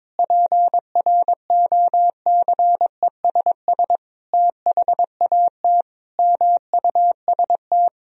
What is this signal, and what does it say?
Morse code, heard as a keyed tone, about 22 wpm, 700 Hz. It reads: PROCESS THAT MUST
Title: Sentences from Top 1000 Words - Review Part 3 - 22wpm